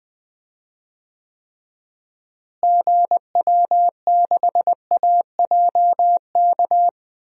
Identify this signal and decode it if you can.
Morse code, heard as a keyed tone, about 20 wpm, 700 Hz. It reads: GW6AJK
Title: DXCC-100 Call Signs - 6 Characters - Repeated 2x - 20wpm